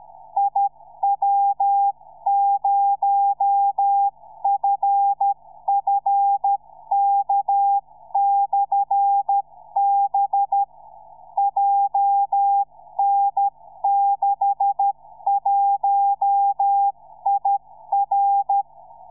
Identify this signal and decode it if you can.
Morse code, heard as a keyed tone, about 13 wpm, 800 Hz. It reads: IW0FFK/B JN61IR
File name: iw0ffk / 2007 04 22 iw0ffk beacon